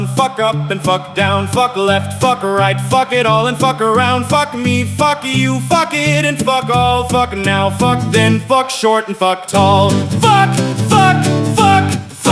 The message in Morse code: none

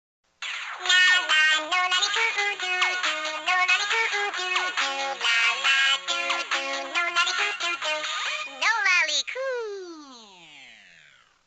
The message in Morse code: none

